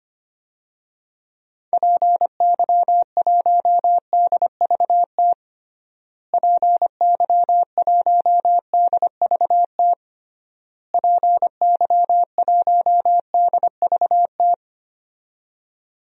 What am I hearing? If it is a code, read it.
Morse code, heard as a keyed tone, about 25 wpm, 700 Hz. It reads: PY1DVT PY1DVT PY1DVT
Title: DXCC-100 Call Signs - 6 Characters - Repeated 3x - 25wpm